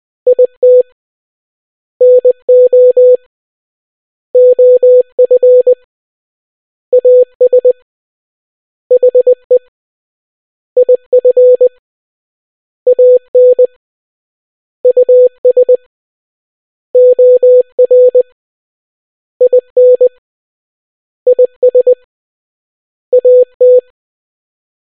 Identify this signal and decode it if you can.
Morse code, heard as a keyed tone, about 20 wpm, 500 Hz. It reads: IT NO OF AS HE IF AN US OR IN IS AT